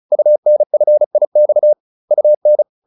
Morse code, heard as a keyed tone, about 34 wpm, 600 Hz. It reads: UNFIX UN